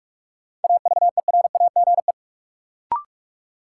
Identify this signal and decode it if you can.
Morse code, heard as a keyed tone, about 46 wpm, 700 Hz. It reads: AVERAGE
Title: Sentences from Top 1400 Words - New Word Drill Alt - 45wpm